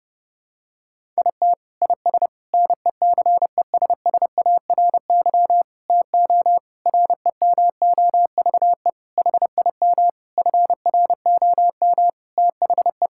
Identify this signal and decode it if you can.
Morse code, heard as a keyed tone, about 32 wpm, 700 Hz. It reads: IT IS NECESSARY TO REMOVE HIM FROM THE
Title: Sentences from Top 1000 Words - Review Part 5 - 30wpm